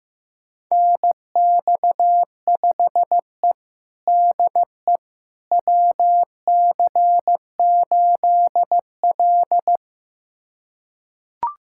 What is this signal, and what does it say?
Morse code, heard as a keyed tone, about 15 wpm, 700 Hz. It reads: NX5E DE WC8L
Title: USA Callsign DE Callsign - 4 Characters - 15wpm